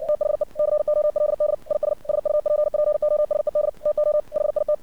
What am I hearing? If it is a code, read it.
Morse code, heard as a keyed tone, about 56 wpm, 600 Hz. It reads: THE COLD IS SUPPOSED TO HIT
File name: archived / CRSnet